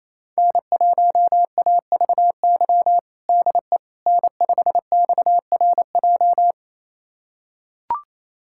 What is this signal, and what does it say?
Morse code, heard as a keyed tone, about 28 wpm, 700 Hz. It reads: N1AVY DE N5XRJ
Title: USA Callsign DE Callsign - 5 Characters - 28wpm